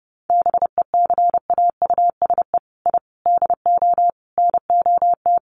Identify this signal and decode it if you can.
Morse code, heard as a keyed tone, about 30 wpm, 700 Hz. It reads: BECAUSE I DO NOT
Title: Sets of 5 Words Encore 30wpm